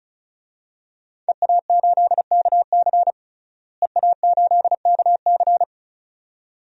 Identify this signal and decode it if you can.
Morse code, heard as a keyed tone, about 34 wpm, 700 Hz. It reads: EA8KC EA8KC
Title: DXCC-100 Call Signs - 5 Characters - Repeated 2x - 35wpm